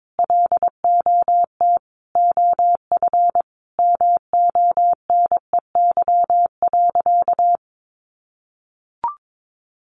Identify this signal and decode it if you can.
Morse code, heard as a keyed tone, about 22 wpm, 700 Hz. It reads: LOT OF MONEY.
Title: Easy Phrases 22wpm